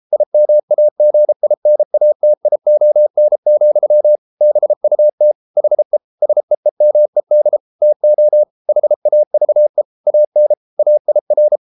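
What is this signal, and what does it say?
Morse code, heard as a keyed tone, about 33 wpm, 600 Hz. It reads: IMAGINATION, BUT HE SEEMED TO HAVE AN AIR
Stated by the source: Morse code practice at 32 wpm from CW med Gnister, reading Ken Follett's novel Never in English.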